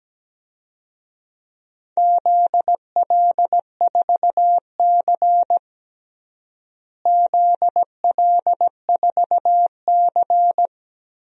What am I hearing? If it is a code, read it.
Morse code, heard as a keyed tone, about 17 wpm, 700 Hz. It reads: ZL4C ZL4C
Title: DXCC-100 Call Signs - 4 Characters - Repeated 2x - 17wpm